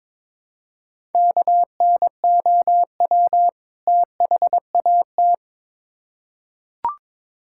Morse code, heard as a keyed tone, about 22 wpm, 700 Hz. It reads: KNOW THAT
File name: Build A Phrase - 2-Word Phrases - 22wpm